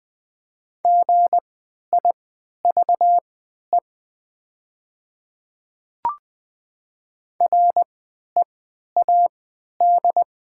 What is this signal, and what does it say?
Morse code, heard as a keyed tone, about 20 wpm, spaced as for 11 wpm, 700 Hz. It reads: GIVE READ
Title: 3x Character Spacing - Top 200 Words - 20wpm